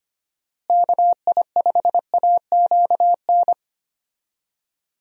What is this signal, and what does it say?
Morse code, heard as a keyed tone, about 25 wpm, 700 Hz. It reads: KI5AQN
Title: USA Call Signs - 6 Characters - Repeated 2x - 25wpm